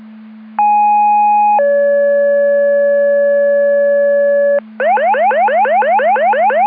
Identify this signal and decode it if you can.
Two-tone page, heard as a one-second tone, then a three-second tone, about 830 then 570 Hz, with noise. It is heard from 0.6 to 4.6 s.